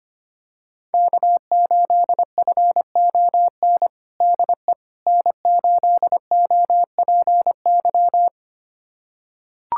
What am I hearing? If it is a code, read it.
Morse code, heard as a keyed tone, about 25 wpm, 700 Hz. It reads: K8FON DE N8OPY
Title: USA Callsign DE Callsign - 5 Characters - 25wpm